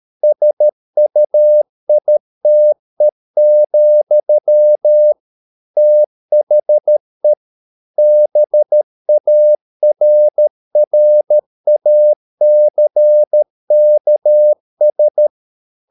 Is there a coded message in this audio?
Yes: Morse code from 0.2 to 15.3 s, heard as a keyed tone, about 13 wpm, 600 Hz. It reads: SUITE, THE BARRACKS